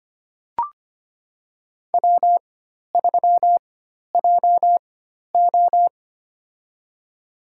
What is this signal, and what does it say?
Morse code, heard as a keyed tone, about 25 wpm, spaced as for 12 wpm, 690 Hz. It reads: W3JO
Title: USA Call Signs - 4 Characters - 4x Character Spacing - 25wpm